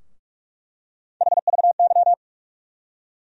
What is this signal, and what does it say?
Morse code, heard as a keyed tone, about 47 wpm, 700 Hz. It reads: HVY
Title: USA Call Sign Postfixes - 3 Letters 45wpm